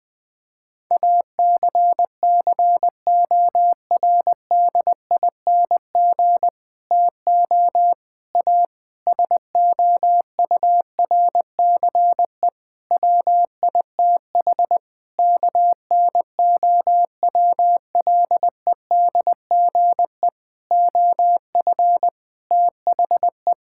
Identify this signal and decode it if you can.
Morse code, heard as a keyed tone, about 20 wpm, 700 Hz. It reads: ACCORDING TO A SOURCE WITH KNOWLEDGE OF THE
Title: Sentences with 9-Word Phrases - Spoken Beginning - 20wpm